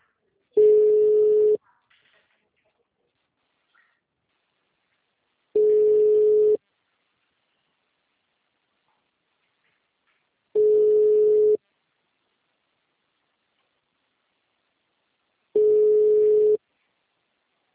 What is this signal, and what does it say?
Ringback tone (Europe, 425 Hz).